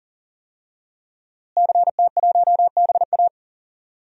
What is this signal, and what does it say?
Morse code, heard as a keyed tone, about 37 wpm, 700 Hz. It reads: CT1BA
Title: DXCC-100 Call Signs - 5 Characters - Repeated 3x - 40wpm